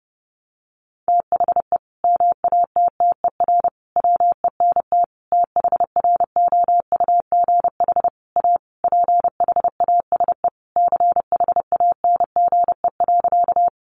Morse code, heard as a keyed tone, about 30 wpm, 700 Hz. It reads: THE MATTER WENT THROUGH A PHASE CHANGE.
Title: Sentences from Top 1200 Words 30wpm